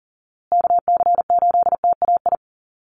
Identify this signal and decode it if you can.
Morse code, heard as a keyed tone, about 40 wpm, 700 Hz. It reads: KC8TAI
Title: USA Call Signs Repeated 3x 40wpm